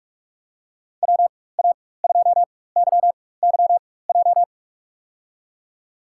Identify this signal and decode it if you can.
Morse code, heard as a keyed tone, about 46 wpm, spaced as for 21 wpm, 700 Hz. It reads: WA2YYJ
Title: USA Call Signs - 6 Characters - 4x Character Spacing - 45wpm